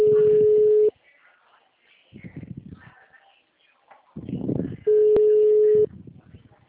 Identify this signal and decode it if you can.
Ringback tone (Europe, 425 Hz), with speech.